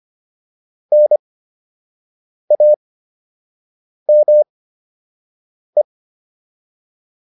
Morse code, heard as a keyed tone, about 25 wpm, spaced as for 6 wpm, 600 Hz. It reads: NAME
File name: QSO 204 06